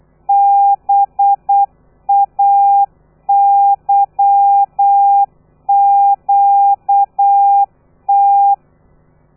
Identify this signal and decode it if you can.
Morse code, heard as a keyed tone, about 8 wpm, 800 Hz. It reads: BAYQT
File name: telegrafia / i2tfj / Lez13 35